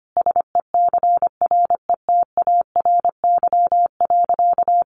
Morse code, heard as a keyed tone, about 25 wpm, 700 Hz. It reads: SECRETARY.
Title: Sentences from Top 1300 Words 25wpm